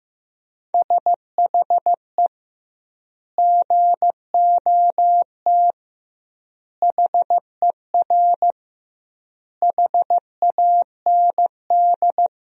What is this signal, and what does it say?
Morse code, heard as a keyed tone, about 16 wpm, 700 Hz. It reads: SHE GOT HER HAND
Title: 2x Word-Spacing - Sentences from Top 500 Words - 15wpm